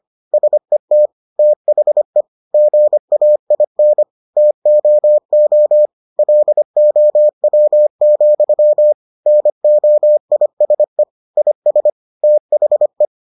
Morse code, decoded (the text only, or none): SET THE GAIN TOO LOW, NOISE IS THE